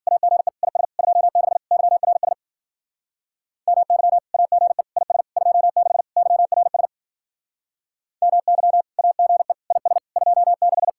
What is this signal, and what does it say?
Morse code, heard as a keyed tone, about 48 wpm, 700 Hz. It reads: AGE IS 26 YRS MY AGE IS 26 YRS MY AGE IS 26
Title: Speed-Racing Common QSO Phrases Encore - 40wpm